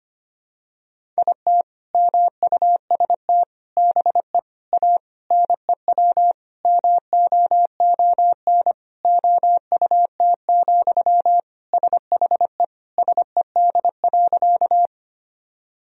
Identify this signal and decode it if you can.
Morse code, heard as a keyed tone, about 25 wpm, 700 Hz. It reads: IT MUST BE A NEW MOON OUT, SHE SED.
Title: Sentences from Top 500 Words and Abbreviations 25wpm